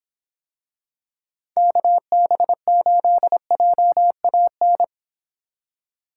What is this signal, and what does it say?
Morse code, heard as a keyed tone, about 26 wpm, 700 Hz. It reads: KB8JAN